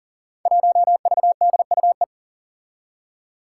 Morse code, heard as a keyed tone, about 37 wpm, 700 Hz. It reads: 1VDUE
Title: Five Challenging Letter-Number Combinations 40wpm